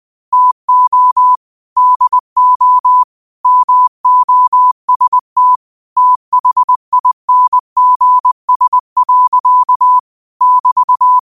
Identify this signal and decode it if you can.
Morse code, heard as a keyed tone, about 20 wpm, 1000 Hz. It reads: TO DO MOST THINGS. <BT>
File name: qotdfiles / 20wpm